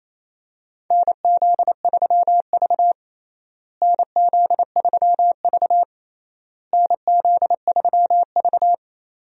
Morse code, heard as a keyed tone, about 27 wpm, 700 Hz. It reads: NZ3V NZ3V NZ3V
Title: USA Call Signs - 4 Characters - Repeated 3x - 28wpm